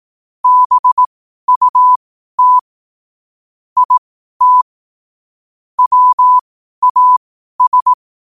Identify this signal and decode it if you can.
Morse code, heard as a keyed tone, about 18 wpm, spaced as for 12 wpm, 1000 Hz. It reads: BUT IT WAS
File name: qotdfiles / 12wpm